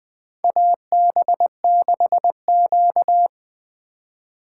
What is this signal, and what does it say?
Morse code, heard as a keyed tone, about 20 wpm, 700 Hz. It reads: AB6Q
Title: USA Call Signs - 4 Characters - 2x Character Spacing - 20wpm